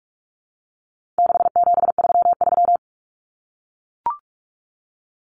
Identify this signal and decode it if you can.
Morse code, heard as a keyed tone, about 44 wpm, 700 Hz. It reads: 6733
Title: Numbers - 4-Digit 45wpm